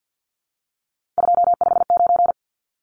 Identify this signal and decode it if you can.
Morse code, heard as a keyed tone, about 50 wpm, 700 Hz. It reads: 259
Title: Numbers - 3-Digit 50wpm